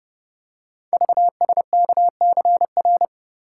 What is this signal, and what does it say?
Morse code, heard as a keyed tone, about 30 wpm, 700 Hz. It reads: VSKCR
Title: Code Groups - 5 Letters - 30wpm